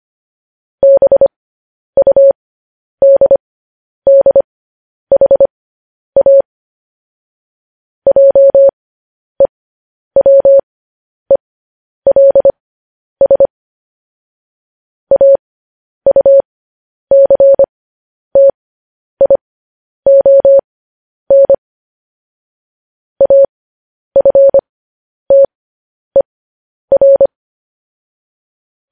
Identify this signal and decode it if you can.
Morse code, heard as a keyed tone, about 25 wpm, spaced as for 10 wpm, 550 Hz. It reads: BUDDHA JEWELS AUCTION AFTER
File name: podcast / News headlines-1x-25x10